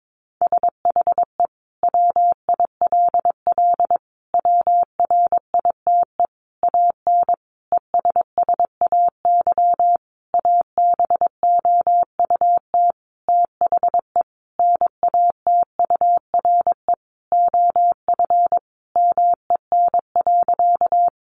Morse code, heard as a keyed tone, about 22 wpm, 700 Hz. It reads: SHE WILL WRITE AN ESSAY ABOUT THE NATURE OF MEN.
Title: Sentences from Top 1500 Words 22wpm